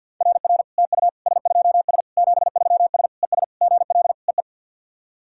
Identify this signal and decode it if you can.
Morse code, heard as a keyed tone, about 51 wpm, 690 Hz. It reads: RR TU S2S 73S ES GL EE
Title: SOTA QSO Practice 50wpm